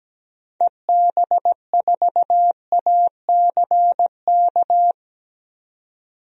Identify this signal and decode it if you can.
Morse code, heard as a keyed tone, about 17 wpm, 700 Hz. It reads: EB4ACK
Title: DXCC-100 Call Signs - 6 Characters - 17wpm